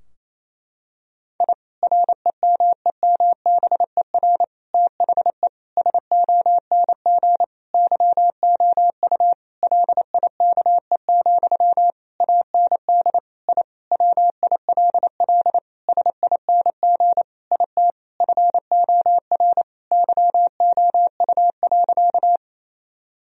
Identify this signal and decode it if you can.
Morse code, heard as a keyed tone, about 28 wpm, 700 Hz. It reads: I REMEMBER THE SONG YOU LIKE, AND I WILL SING IT FOR YOU.
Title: Sentences from Top 1000 Words - Review Part 2 - 28wpm